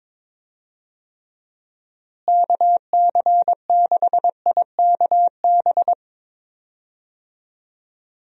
Morse code, heard as a keyed tone, about 22 wpm, 700 Hz. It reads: KC6IKB
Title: USA Call Signs - 6 Characters - Repeated 3x - 22wpm